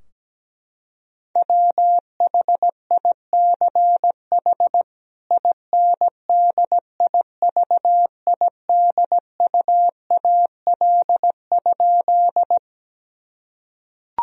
Morse code, heard as a keyed tone, about 17 wpm, 700 Hz. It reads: WHICH INDIVIDUAL?
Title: Sentences from Top 1000 Words - Review Part 4 - 17wpm